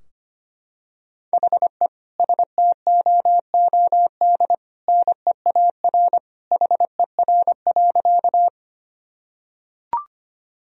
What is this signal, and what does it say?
Morse code, heard as a keyed tone, about 25 wpm, 700 Hz. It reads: HE STOOD NEAR HER.
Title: Sentences from Top 1000 Words - Review Part 2 - 25wpm